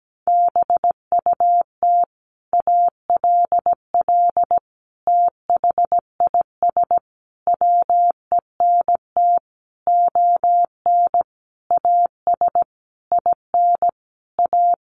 Morse code, heard as a keyed tone, about 17 wpm, 700 Hz. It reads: BUT ALL THIS WENT ON AS IN A
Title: Sentences from Top 1500 Words 17wpm